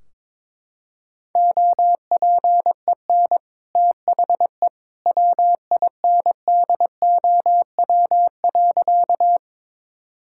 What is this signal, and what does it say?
Morse code, heard as a keyed tone, about 22 wpm, 700 Hz. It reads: OPEN THE WINDOW.
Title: Sentences from Top 1600 Words 22wpm